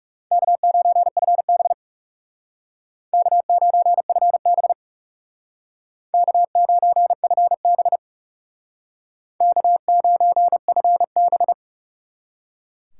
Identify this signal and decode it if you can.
Morse code, heard as a keyed tone, about 40 wpm, 700 Hz. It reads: K9FB K9FB K9FB K9FB